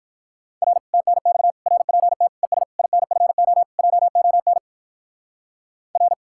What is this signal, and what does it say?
Morse code, heard as a keyed tone, about 58 wpm, 700 Hz. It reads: R TNX RPT ES INFO JON R